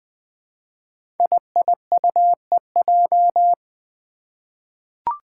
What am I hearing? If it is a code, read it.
Morse code, heard as a keyed tone, about 20 wpm, 700 Hz. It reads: IIUEJ
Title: Five Challenging Letter-Number Combinations 20wpm